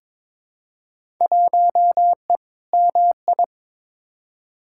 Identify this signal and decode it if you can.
Morse code, heard as a keyed tone, about 22 wpm, 700 Hz. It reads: 1E MI
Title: Field Day Exchanges - 22wpm